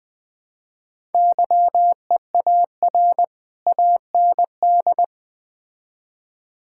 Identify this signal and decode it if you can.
Morse code, heard as a keyed tone, about 20 wpm, 700 Hz. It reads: YEAR AND
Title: Build A Phrase - 4-Word Phrases - 20wpm